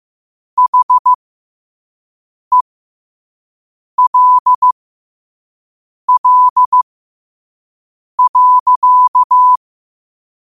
Morse code, heard as a keyed tone, about 15 wpm, spaced as for 5 wpm, 1000 Hz. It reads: HELL.
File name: qotdfiles / 5wpm